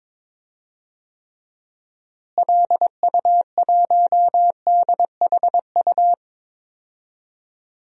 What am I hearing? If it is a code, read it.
Morse code, heard as a keyed tone, about 22 wpm, 700 Hz. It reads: LU1DHU
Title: DXCC-100 Call Signs - 6 Characters - Repeated 2x - 22wpm